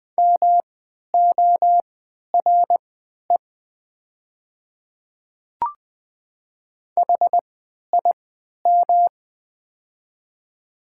Morse code, heard as a keyed tone, about 20 wpm, spaced as for 11 wpm, 700 Hz. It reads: MORE HIM
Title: 3x Character Spacing - Top 100 Words - 20wpm